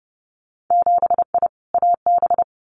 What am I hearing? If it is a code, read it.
Morse code, heard as a keyed tone, about 30 wpm, 700 Hz. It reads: 7I AB
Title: Winter Field Day Exchanges Encore 30wpm